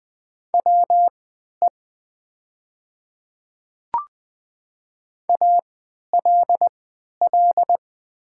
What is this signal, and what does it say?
Morse code, heard as a keyed tone, about 20 wpm, spaced as for 11 wpm, 690 Hz. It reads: WE ALL
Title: 3x Character Spacing - Top 300 Words - 20wpm